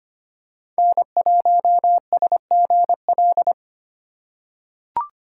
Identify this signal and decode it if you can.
Morse code, heard as a keyed tone, about 25 wpm, 700 Hz. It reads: N1SGL
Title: USA Call Signs - 5 Characters - 4x Character Spacing - 25wpm